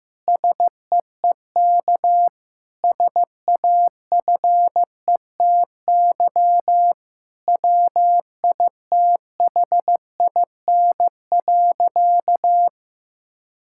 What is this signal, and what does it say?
Morse code, heard as a keyed tone, about 15 wpm, 700 Hz. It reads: SEEK SAFETY WITHIN.
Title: Sentences from Top 1600 Words 15wpm